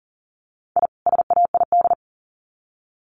Morse code, heard as a keyed tone, about 40 wpm, 700 Hz. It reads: I SAID
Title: Sets of 2 Words 40wpm